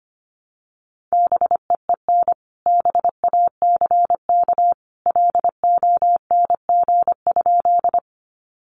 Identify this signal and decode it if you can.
Morse code, heard as a keyed tone, about 25 wpm, 700 Hz. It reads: BEEN BACK LONG?
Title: Sentences from Top 200 Words 25wpm